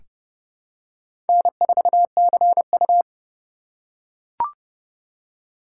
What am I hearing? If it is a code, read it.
Morse code, heard as a keyed tone, about 31 wpm, 700 Hz. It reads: N4CU
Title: USA Call Signs - 4 Characters - 4x Character Spacing - 30wpm